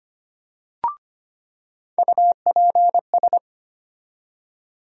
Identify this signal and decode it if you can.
Morse code, heard as a keyed tone, about 25 wpm, 700 Hz. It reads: UPS
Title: USA Call Sign Postfixes - 3 Letters 25wpm